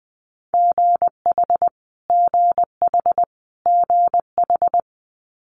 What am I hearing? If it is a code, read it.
Morse code, heard as a keyed tone, about 20 wpm, 700 Hz. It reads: GH GH GH
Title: Multi-letter Phonemes Repeated 3x 20wpm